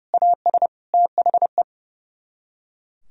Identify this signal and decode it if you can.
Morse code, heard as a keyed tone, about 35 wpm, 700 Hz. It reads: AS THE
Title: Speed-Racing - Sets of 4 Words 30wpm